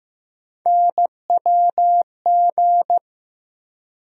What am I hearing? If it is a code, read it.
Morse code, heard as a keyed tone, about 15 wpm, 700 Hz. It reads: NWG
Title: USA Call Sign Postfixes - 3 Letters 15wpm